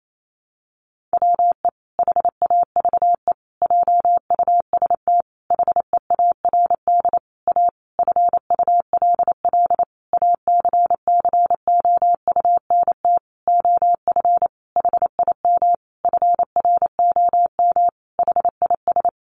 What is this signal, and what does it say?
Morse code, heard as a keyed tone, about 28 wpm, 700 Hz. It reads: WE HAVE JUST HEARD A FULL ACCOUNT OF HIM FROM HIS
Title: Sentences from Top 900 Words 28wpm